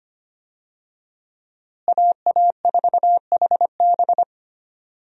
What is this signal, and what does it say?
Morse code, heard as a keyed tone, about 25 wpm, 700 Hz. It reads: AA4HB
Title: USA Call Signs - 5 Characters - Repeated 2x - 25wpm